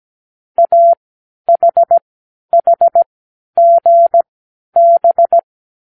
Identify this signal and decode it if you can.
Morse code, heard as a keyed tone, about 17 wpm, spaced as for 10 wpm, 680 Hz. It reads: AHHGB
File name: k7qo / Lesson12(A-H)